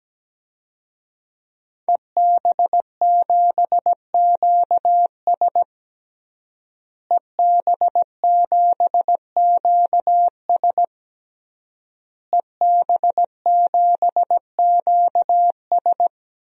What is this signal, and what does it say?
Morse code, heard as a keyed tone, about 17 wpm, 700 Hz. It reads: EB7QS EB7QS EB7QS
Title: DXCC-100 Call Signs - 5 Characters - Repeated 3x - 17wpm